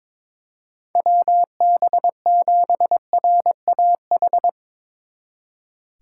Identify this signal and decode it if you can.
Morse code, heard as a keyed tone, about 22 wpm, 700 Hz. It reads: WB7RAH
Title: Speed Racing - USA 2x3 Call Signs 22wpm